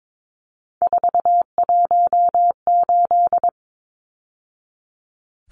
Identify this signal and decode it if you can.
Morse code, heard as a keyed tone, about 22 wpm, 700 Hz. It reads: 418
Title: Numbers - 3-Digit 22wpm